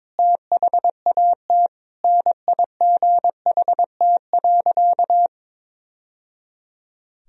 Morse code, heard as a keyed tone, about 21 wpm, 700 Hz. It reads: THAT NIGHT.
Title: Sentences from Top 1000 Words - Review Part 3 - 22wpm